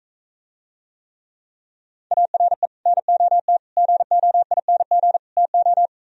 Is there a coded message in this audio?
Yes: Morse code from 2.1 to 5.9 s, heard as a keyed tone, about 44 wpm, 700 Hz. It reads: ARE NOT GOING TO